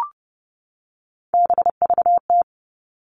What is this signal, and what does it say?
Morse code, heard as a keyed tone, about 30 wpm, 700 Hz. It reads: BVT
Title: All Cut Numbers - 3-Digit 30wpm